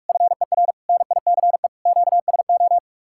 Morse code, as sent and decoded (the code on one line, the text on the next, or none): ..-. .-. / -. .. -.-. / --.- ... ---
FER NICE QSO